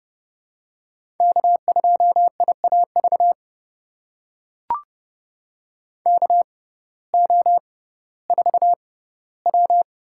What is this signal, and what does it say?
Morse code, heard as a keyed tone, about 31 wpm, spaced as for 17 wpm, 700 Hz. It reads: K2IAV KO4W
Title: USA Call Signs - 5 Characters - 6x Character Spacing - 30wpm